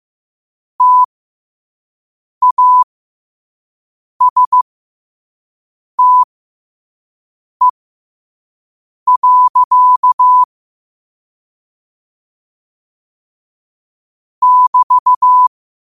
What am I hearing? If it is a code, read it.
Morse code, heard as a keyed tone, about 15 wpm, spaced as for 5 wpm, 1000 Hz. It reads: TASTE. <BT>
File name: qotdfiles / 5wpm